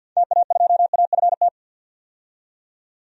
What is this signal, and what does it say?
Morse code, heard as a keyed tone, about 52 wpm, 700 Hz. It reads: TA2AFT
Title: DXCC-100 Call Signs - 6 Characters - 50wpm